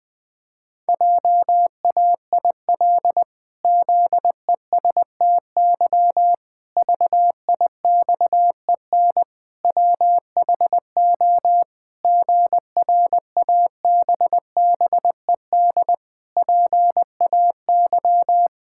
Morse code, heard as a keyed tone, about 20 wpm, 700 Hz. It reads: JAIL ZESTY VIXEN WHO GRABBED PAY
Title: Pangram Challenge v2 - 20wpm